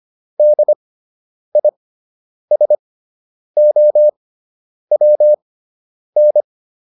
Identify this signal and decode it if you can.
Morse code, heard as a keyed tone, about 25 wpm, spaced as for 9 wpm, 600 Hz. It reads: DISOWN